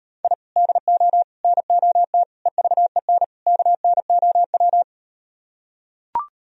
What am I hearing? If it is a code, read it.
Morse code, heard as a keyed tone, about 38 wpm, 700 Hz. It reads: I DO NOT EVEN KNOW